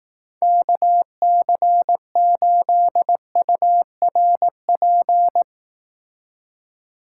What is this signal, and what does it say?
Morse code, heard as a keyed tone, about 18 wpm, 700 Hz. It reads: KC8URP
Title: Speed Racing - USA 2x3 Call Signs 15wpm